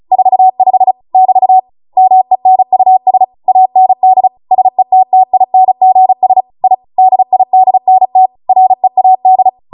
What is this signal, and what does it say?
Morse code, heard as a keyed tone, about 36 wpm, 750 Hz. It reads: 45 <BT> MENUS AND SETTINGS I DIDNT READ